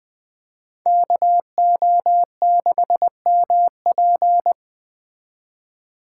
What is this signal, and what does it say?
Morse code, heard as a keyed tone, about 21 wpm, 700 Hz. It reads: KO6MP